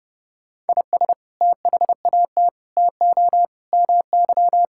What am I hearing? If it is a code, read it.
Morse code, heard as a keyed tone, about 32 wpm, 700 Hz. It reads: IS THAT TO MY
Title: Sentences from Top 1000 Words - Review Part 6 - 30wpm